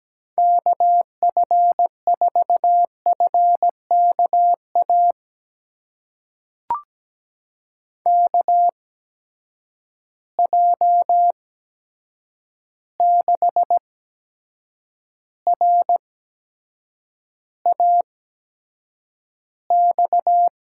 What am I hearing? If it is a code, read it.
Morse code, heard as a keyed tone, about 17 wpm, spaced as for 7 wpm, 700 Hz. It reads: KF4FKA KJ6RAX
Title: USA Call Signs - 6 Characters - 8x Character Spacing - 17wpm